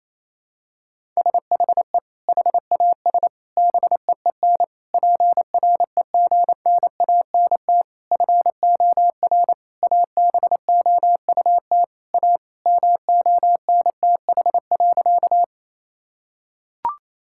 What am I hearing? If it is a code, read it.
Morse code, heard as a keyed tone, about 28 wpm, 700 Hz. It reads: SHE HAS BEEN PREGNANT FOR ABOUT A MONTH.
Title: Sentences from Top 1000 Words - Review Part 6 - 28wpm